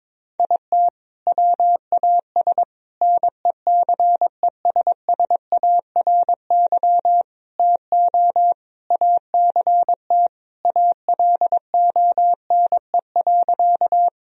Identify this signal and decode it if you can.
Morse code, heard as a keyed tone, about 22 wpm, 700 Hz. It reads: IT WAS NECESSARY TO ACT ALONE.
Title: Sentences from Top 1000 Words - Review Part 1 - 22wpm